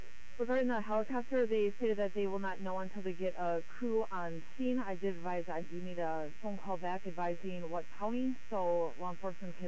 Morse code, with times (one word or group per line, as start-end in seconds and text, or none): none